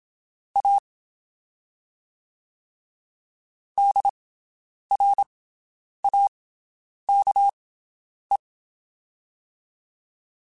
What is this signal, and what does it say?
Morse code, heard as a keyed tone, about 27 wpm, spaced as for 9 wpm, 800 Hz. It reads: A DRAKE
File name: QSOs Spaced